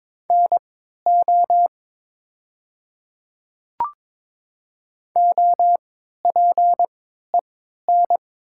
Morse code, heard as a keyed tone, about 22 wpm, spaced as for 12 wpm, 700 Hz. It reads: NO OPEN
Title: 3x Character Spacing - Top 300 Words - 22wpm